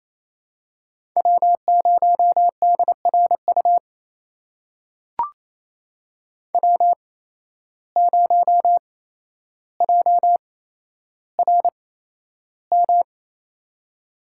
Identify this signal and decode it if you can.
Morse code, heard as a keyed tone, about 28 wpm, spaced as for 12 wpm, 700 Hz. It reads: W0DRU W0JRM
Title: USA Call Signs - 5 Characters - 8x Character Spacing - 28wpm